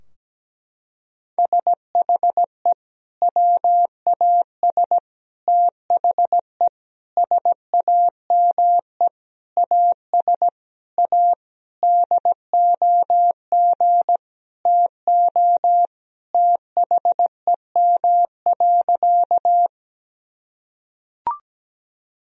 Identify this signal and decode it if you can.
Morse code, heard as a keyed tone, about 17 wpm, 700 Hz. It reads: SHE WAS THE SAME AS A DOG TO THEM.
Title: Sentences from Top 1000 Words - Review Part 2 - 17wpm